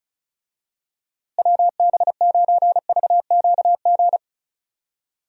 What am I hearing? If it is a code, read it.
Morse code, heard as a keyed tone, about 36 wpm, 700 Hz. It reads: WB9VQG
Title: USA Call Signs - 6 Characters - Repeated 2x - 35wpm